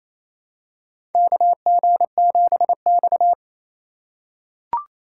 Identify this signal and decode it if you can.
Morse code, heard as a keyed tone, about 28 wpm, 700 Hz. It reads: KG7X
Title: USA Call Signs - 4 Characters - 4x Character Spacing - 28wpm